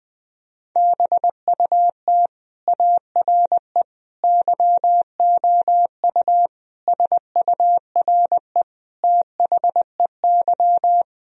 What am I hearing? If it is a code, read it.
Morse code, heard as a keyed tone, about 21 wpm, 700 Hz. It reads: BUT ARE YOU SURE THEY
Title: Sentences from Top 1000 Words - Review Part 1 - 20wpm